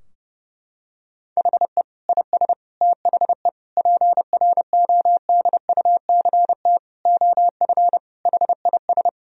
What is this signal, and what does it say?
Morse code, heard as a keyed tone, about 30 wpm, 700 Hz. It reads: HE IS THE PRODUCT OF HIS
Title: Sentences from Top 1000 Words - Review Part 2 - 30wpm